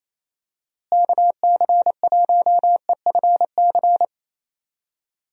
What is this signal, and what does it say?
Morse code, heard as a keyed tone, about 28 wpm, 700 Hz. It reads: KC1EFC
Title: USA Call Signs - 6 Characters - Repeated 3x - 28wpm